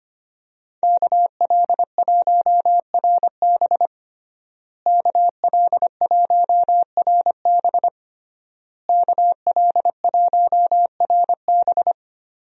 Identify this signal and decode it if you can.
Morse code, heard as a keyed tone, about 25 wpm, 700 Hz. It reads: KL1RB KL1RB KL1RB
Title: USA Call Signs - 5 Characters - Repeated 3x - 25wpm